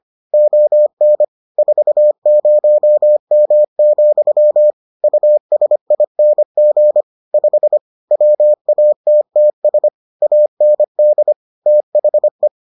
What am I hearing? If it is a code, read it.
Morse code, heard as a keyed tone, about 25 wpm, 600 Hz. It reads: ON 40M, USING 5 WATTS AND THE